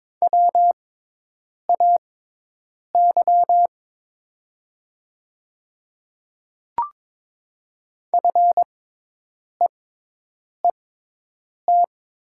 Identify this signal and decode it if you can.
Morse code, heard as a keyed tone, about 22 wpm, spaced as for 8 wpm, 700 Hz. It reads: WAY FEET
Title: 6x Character Spacing - Top 300 Words - 22wpm